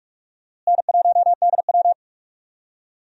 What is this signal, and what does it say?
Morse code, heard as a keyed tone, about 44 wpm, 700 Hz. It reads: N1DW
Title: USA Call Signs - 4 Characters - Repeated 2x - 45wpm